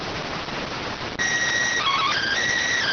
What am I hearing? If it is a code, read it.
SSTV (slow-scan television), PD 290.